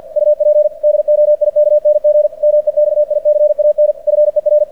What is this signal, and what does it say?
Morse code, heard as a keyed tone, about 46 wpm, 600 Hz. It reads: UP DOIKNG MEXICAN FER
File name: archived / CRSnet